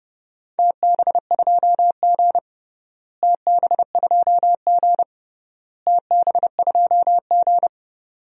Morse code, heard as a keyed tone, about 30 wpm, 700 Hz. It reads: TB2G TB2G TB2G
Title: DXCC-100 Call Signs - 4 Characters - Repeated 3x - 30wpm